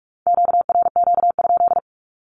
Morse code, heard as a keyed tone, about 46 wpm, 700 Hz. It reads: QRQ?
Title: Q-Codes with Spoken Meaning - 45wpm